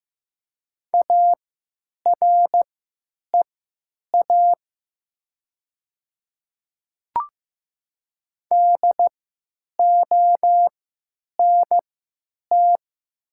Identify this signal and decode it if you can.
Morse code, heard as a keyed tone, about 15 wpm, spaced as for 9 wpm, 700 Hz. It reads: AREA DONT
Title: 3x Character Spacing - Top 500 Words - 15wpm